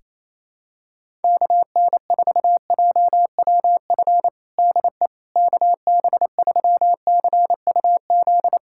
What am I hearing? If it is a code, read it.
Morse code, heard as a keyed tone, about 28 wpm, 700 Hz. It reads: KN4JWF DE KB3CUZ
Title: USA Callsign DE Callsign 28wpm